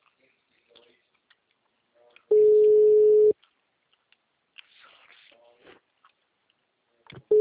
Ringback tone (Europe, 425 Hz).